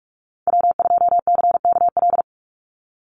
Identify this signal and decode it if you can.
Morse code, heard as a keyed tone, about 44 wpm, 700 Hz. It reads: W2CKL
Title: NA SOTA Activator Callsigns 45wpm